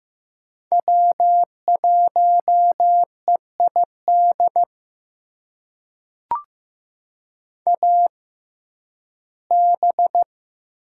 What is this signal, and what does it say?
Morse code, heard as a keyed tone, about 15 wpm, spaced as for 11 wpm, 700 Hz. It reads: W1EID AB
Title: USA Call Signs - 6x Character Spacing - 15wpm